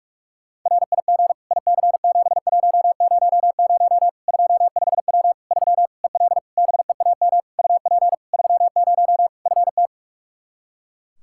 Morse code, heard as a keyed tone, about 46 wpm, 700 Hz. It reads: RIG IC7100 25W 3 EL BEAM UP 30 FT
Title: Practice QSO - Exchange 2 - Info - 45wpm